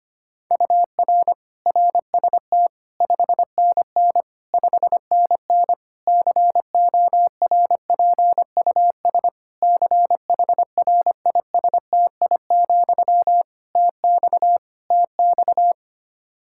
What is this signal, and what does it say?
Morse code, heard as a keyed tone, about 25 wpm, 700 Hz. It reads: UR RST 5NN 5NN CORPUS CHRISTI, TX TX
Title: K1USN SST QSO Practice 25wpm